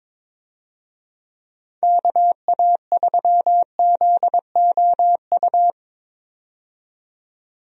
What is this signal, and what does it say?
Morse code, heard as a keyed tone, about 22 wpm, 700 Hz. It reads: KA3ZOU
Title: USA Call Signs - 6 Characters - 22wpm